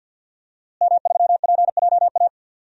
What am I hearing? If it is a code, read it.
Morse code, heard as a keyed tone, about 52 wpm, 700 Hz. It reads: M3PJA
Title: DXCC-100 Call Signs - 5 Characters - Repeated 2x - 50wpm